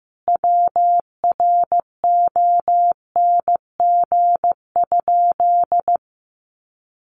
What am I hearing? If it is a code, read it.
Morse code, heard as a keyed tone, about 15 wpm, 700 Hz. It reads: WRONG?
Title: A Sound Of Thunder 15wpm